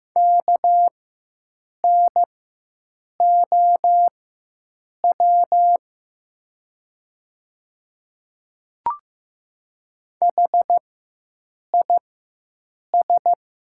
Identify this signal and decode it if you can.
Morse code, heard as a keyed tone, about 15 wpm, spaced as for 7 wpm, 700 Hz. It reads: KNOW HIS
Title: 4x Character Spacing - Top 100 Words - 15wpm